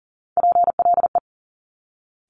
Morse code, heard as a keyed tone, about 42 wpm, 700 Hz. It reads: PLE
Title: Top Three Letter Combinations in Words 40wpm